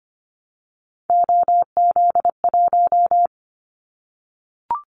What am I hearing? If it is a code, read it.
Morse code, heard as a keyed tone, about 25 wpm, 700 Hz. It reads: OZ1
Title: Callsign Prefixes for Top 100 DXCC - 25wpm